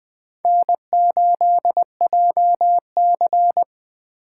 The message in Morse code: N8JC

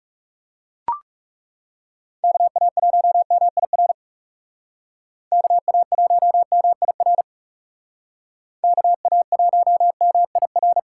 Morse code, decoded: KA1MIR KA1MIR KA1MIR

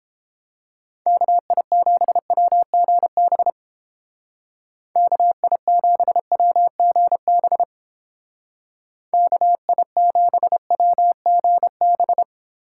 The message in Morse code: KI7WGB KI7WGB KI7WGB

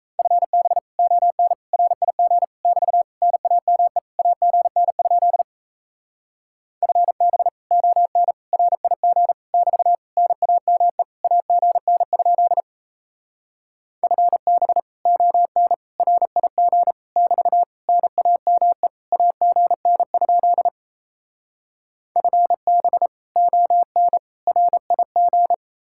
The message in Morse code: FB ON RIG <BT> NAME AGN? FB ON RIG <BT> NAME AGN? FB ON RIG <BT> NAME AGN? FB ON RIG

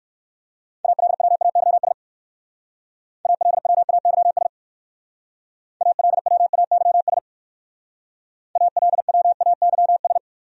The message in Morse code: ALWAYS ALWAYS ALWAYS ALWAYS

